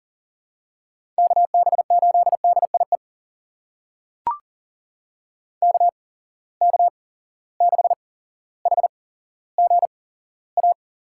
Morse code, heard as a keyed tone, about 42 wpm, spaced as for 17 wpm, 700 Hz. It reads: KB8DIE KK6HGA
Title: USA Call Signs - 6 Characters - 8x Character Spacing - 40wpm